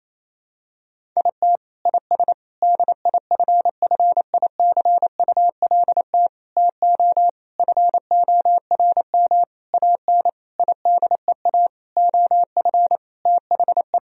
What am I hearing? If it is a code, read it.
Morse code, heard as a keyed tone, about 27 wpm, 700 Hz. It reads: IT IS DIFFICULT TO FORM AN IDEA OF THE
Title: Sentences from Top 1000 Words - Review Part 1 - 28wpm